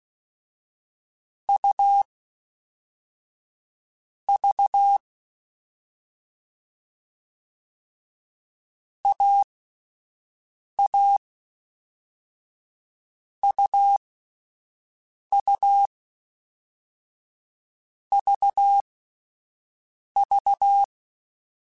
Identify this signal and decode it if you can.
Morse code, heard as a keyed tone, about 16 wpm, spaced as for 5 wpm, 800 Hz. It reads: U V AA UU VV
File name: Difficult Combos / Difficult Combos-mp3s / AUV AUV4 @5